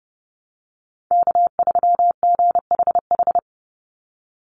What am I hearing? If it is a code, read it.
Morse code, heard as a keyed tone, about 30 wpm, 700 Hz. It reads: K3GHH
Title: Call Signs - 30wpm